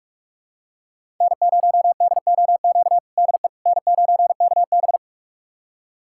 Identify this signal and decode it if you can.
Morse code, heard as a keyed tone, about 46 wpm, 700 Hz. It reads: N0DOQ DE N9KB
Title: USA Callsign DE Callsign 45wpm